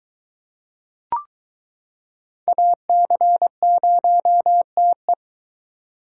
Morse code, heard as a keyed tone, about 22 wpm, 700 Hz. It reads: AC0TE